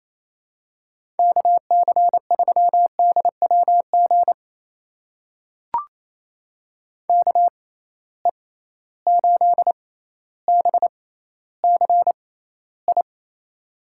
Morse code, heard as a keyed tone, about 28 wpm, spaced as for 14 wpm, 700 Hz. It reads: KC3DWG KE8BCI
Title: USA Call Signs - 6 Characters - 6x Character Spacing - 28wpm